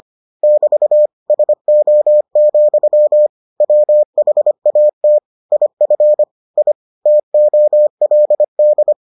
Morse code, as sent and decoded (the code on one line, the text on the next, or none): -...- / ... --- --..-- / .-- .... .- - / .. ..-. / .. / - --- .-.. -..
<BT> SO, WHAT IF I TOLD